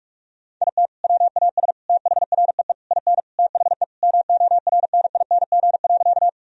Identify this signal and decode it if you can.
Morse code, heard as a keyed tone, about 41 wpm, 700 Hz. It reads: IT WAS THREE IN THE MORNING.
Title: Sentences from Top 1000 Words - Review Part 1 - 45wpm